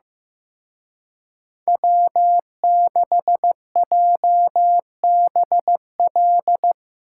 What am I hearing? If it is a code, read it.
Morse code, heard as a keyed tone, about 17 wpm, 700 Hz. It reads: W6JBL
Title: Speed Racing - USA 1x3 Call Signs 15wpm